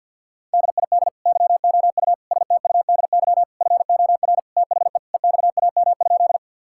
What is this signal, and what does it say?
Morse code, heard as a keyed tone, about 50 wpm, 700 Hz. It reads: DID YOU STUDY FOR THE EXAM?
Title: Sentences from Top 1000 Words - Review Part 5 - 50wpm